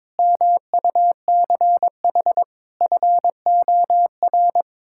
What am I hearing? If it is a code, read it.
Morse code, heard as a keyed tone, about 22 wpm, 700 Hz. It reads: MUCH FOR
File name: practice / 22wpm / Build A Phrase - 5-Word Phrases - 22wpm